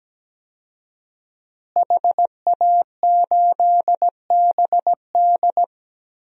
Morse code, heard as a keyed tone, about 17 wpm, 700 Hz. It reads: HA8BD
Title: DXCC-100 Call Signs - 5 Characters - Repeated 2x - 17wpm